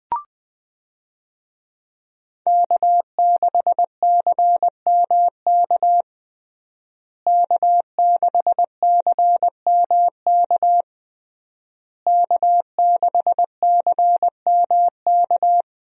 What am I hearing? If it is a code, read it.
Morse code, heard as a keyed tone, about 20 wpm, 700 Hz. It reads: K6CMK K6CMK K6CMK
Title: USA Call Signs - 5 Characters - Repeated 3x - 20wpm